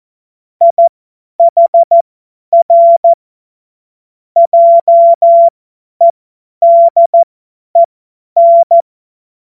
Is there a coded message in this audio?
Yes: Morse code from 0.6 to 8.8 s, heard as a keyed tone, about 14 wpm, spaced as for 10 wpm, 680 Hz. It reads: IHR JEDEN